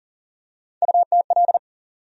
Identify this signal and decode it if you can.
Morse code, heard as a keyed tone, about 40 wpm, 700 Hz. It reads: UTL
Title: USA Call Sign Postfixes - 3 Letters 40wpm